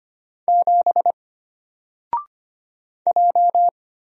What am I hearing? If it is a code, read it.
Morse code, heard as a keyed tone, about 25 wpm, 700 Hz. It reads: 7 J
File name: Single-Letter-Number-25wpm